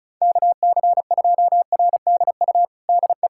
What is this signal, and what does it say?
Morse code, heard as a keyed tone, about 34 wpm, 690 Hz. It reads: KC2RDU DE
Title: USA Callsign DE Callsign 35wpm